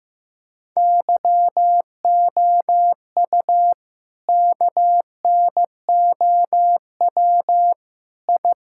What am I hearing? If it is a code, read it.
Morse code, heard as a keyed tone, about 16 wpm, 700 Hz. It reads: YOU KNOW I
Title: Sentences from Top 1000 Words - Review Part 6 - 15wpm